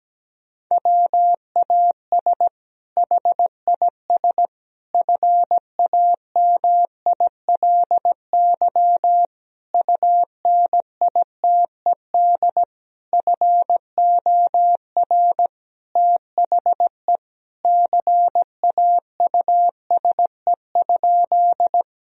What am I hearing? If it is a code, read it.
Morse code, heard as a keyed tone, about 17 wpm, 700 Hz. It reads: WAS HIS FAMILY UNITED FOR THE CAUSE?
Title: Sentences from Top 1000 Words - Review Part 4 - 17wpm